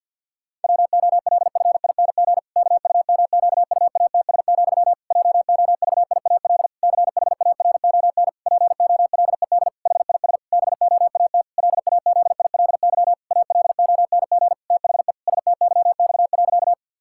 Morse code, heard as a keyed tone, about 48 wpm, 700 Hz. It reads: WOLFING KUMQUATS, JOVIAL CHARON POLED HIS BOAT LAZILY ALONG THE STYX.
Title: Pangram Challenge v2 - 50wpm